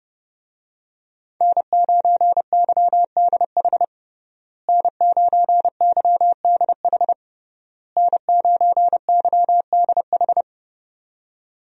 Morse code, heard as a keyed tone, about 30 wpm, 700 Hz. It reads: N9YDH N9YDH N9YDH
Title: USA Call Signs - 5 Characters - Repeated 3x - 30wpm